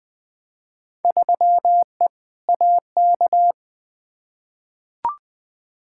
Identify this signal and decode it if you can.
Morse code, heard as a keyed tone, about 21 wpm, 700 Hz. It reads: 3E AK
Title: Field Day Exchanges Encore - 20wpm